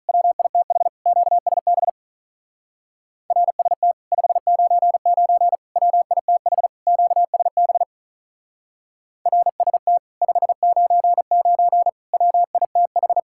Morse code, heard as a keyed tone, about 46 wpm, 700 Hz. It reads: WITH QSB RST 599 WITH QSB RST 599 WITH